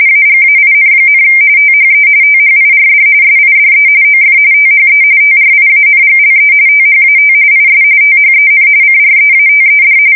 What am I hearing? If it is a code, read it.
RTTY, about 45.45 baud. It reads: AGAIN, THE THIRD SIDE OF THE TRIANGLE THAT IS THEN CREATED
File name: fsk170 / printing-telegraph